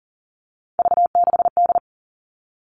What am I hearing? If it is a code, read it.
Morse code, heard as a keyed tone, about 40 wpm, 700 Hz. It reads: V6D
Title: All Cut Numbers - 3-Digit 40wpm